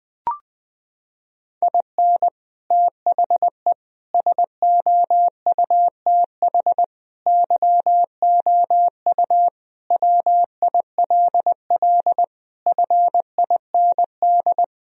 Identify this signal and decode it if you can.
Morse code, heard as a keyed tone, about 20 wpm, 700 Hz. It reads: IN THE SOUTH YOU WILL FIND
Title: Sentences from Top 1000 Words - Review Part 2 - 20wpm